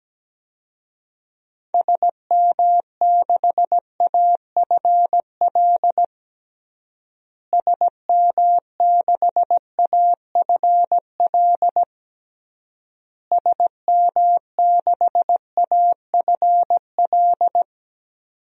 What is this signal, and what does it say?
Morse code, heard as a keyed tone, about 17 wpm, 700 Hz. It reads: SM6AFL SM6AFL SM6AFL
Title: DXCC-100 Call Signs - 6 Characters - Repeated 3x - 17wpm